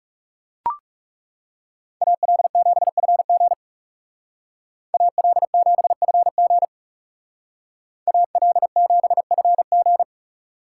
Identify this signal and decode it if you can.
Morse code, heard as a keyed tone, about 37 wpm, 700 Hz. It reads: AL7FG AL7FG AL7FG
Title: Speed Racing - USA 2x2 Call Signs 30wpm